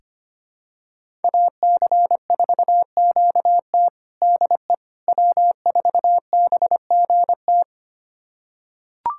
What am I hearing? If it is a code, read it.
Morse code, heard as a keyed tone, about 25 wpm, 700 Hz. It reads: AC4QT DE W4BGT
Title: USA Callsign DE Callsign - 5 Characters - 25wpm